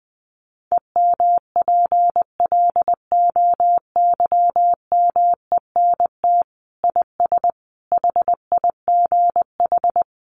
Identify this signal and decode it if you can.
Morse code, heard as a keyed tone, about 20 wpm, 700 Hz. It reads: EMPLOYMENT IS HIGH
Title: Sentences from Top 1500 Words 20wpm